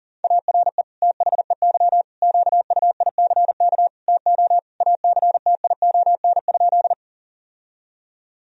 Morse code, heard as a keyed tone, about 40 wpm, 700 Hz. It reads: ARE THEY QUICK TO ACTION?
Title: Sentences from Top 1000 Words - Review Part 4 - 40wpm